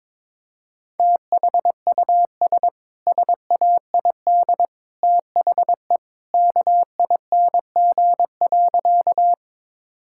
Morse code, heard as a keyed tone, about 22 wpm, 700 Hz. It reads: THUS SAID THE KING.
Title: Sentences from Top 1000 Words - Review Part 5 - 22wpm